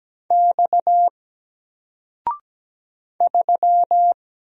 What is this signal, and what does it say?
Morse code, heard as a keyed tone, about 17 wpm, 700 Hz. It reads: X 3
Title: Single Letter-Number 17wpm